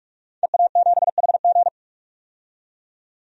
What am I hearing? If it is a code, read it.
Morse code, heard as a keyed tone, about 46 wpm, 700 Hz. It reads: EA7HG
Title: DXCC-100 Call Signs - 5 Characters - 45wpm